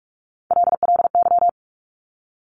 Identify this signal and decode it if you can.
Morse code, heard as a keyed tone, about 46 wpm, 700 Hz. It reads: LLY
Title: Top Three Letter Combinations in Words 45wpm